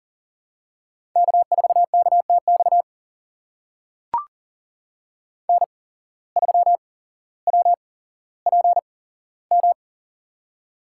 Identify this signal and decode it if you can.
Morse code, heard as a keyed tone, about 40 wpm, spaced as for 17 wpm, 700 Hz. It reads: K4KTX N3WPM
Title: USA Call Signs - 5 Characters - 8x Character Spacing - 40wpm